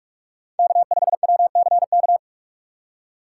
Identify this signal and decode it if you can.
Morse code, heard as a keyed tone, about 46 wpm, 700 Hz. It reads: K5WCK
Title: USA Call Signs - 5 Characters - 3x Character Spacing - 45wpm